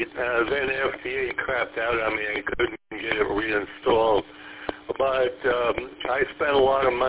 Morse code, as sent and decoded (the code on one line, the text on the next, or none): none